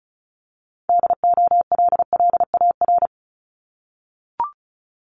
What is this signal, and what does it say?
Morse code, heard as a keyed tone, about 36 wpm, 700 Hz. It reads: DOLLAR
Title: Multi-letter Phonemes 35wpm